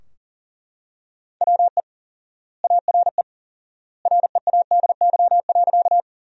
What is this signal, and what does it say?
Morse code, heard as a keyed tone, about 37 wpm, 700 Hz. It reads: WE ARE READY.
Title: 4x Word-Spacing - Sentences from Top 400 Words - 40wpm